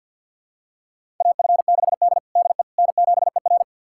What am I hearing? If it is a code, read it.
Morse code, heard as a keyed tone, about 49 wpm, 700 Hz. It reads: AF6D DE N7ER
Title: USA Callsign DE Callsign - 4 Characters - 50wpm